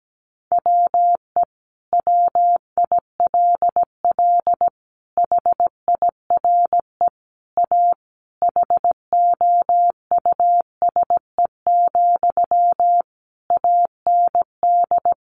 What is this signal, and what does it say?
Morse code, heard as a keyed tone, about 17 wpm, 700 Hz. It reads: WE WILL HIRE A HOUSE, AND
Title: Sentences from Top 1400 Words 17wpm